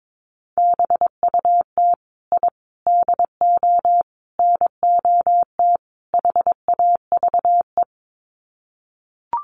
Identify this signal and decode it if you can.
Morse code, heard as a keyed tone, about 22 wpm, 700 Hz. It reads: BUT I DO NOT HAVE
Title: Sets of 5 Words Encore 22wpm